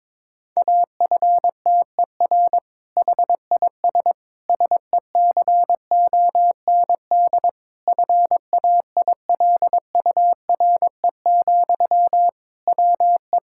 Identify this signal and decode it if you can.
Morse code, heard as a keyed tone, about 22 wpm, 700 Hz. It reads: AFTER HIS SECOND FAILURE, WE
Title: Sentences from Top 1000 Words - Review Part 6 - 22wpm